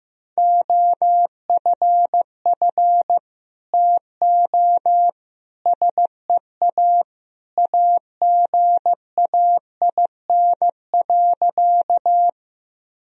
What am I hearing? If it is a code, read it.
Morse code, heard as a keyed tone, about 15 wpm, 700 Hz. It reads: OFF TO SEA AGAIN.
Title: Sentences from Top 1000 Words - Review Part 1 - 15wpm